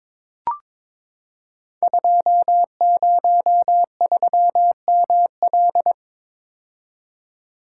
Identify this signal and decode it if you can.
Morse code, heard as a keyed tone, about 23 wpm, 700 Hz. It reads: 203ML
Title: Code Groups - 5 Letter:Numbers - 22wpm